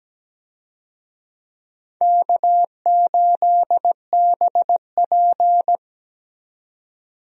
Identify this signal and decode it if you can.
Morse code, heard as a keyed tone, about 17 wpm, 700 Hz. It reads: K8BP
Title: USA Call Signs - 4 Characters - 17wpm